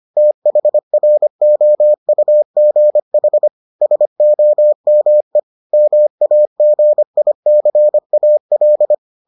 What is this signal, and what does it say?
Morse code, heard as a keyed tone, about 25 wpm, 600 Hz. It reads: THROUGH SOME MAGICAL